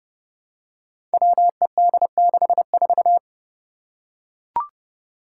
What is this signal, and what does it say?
Morse code, heard as a keyed tone, about 30 wpm, 700 Hz. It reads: WED64
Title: Five Challenging Letter-Number Combinations 30wpm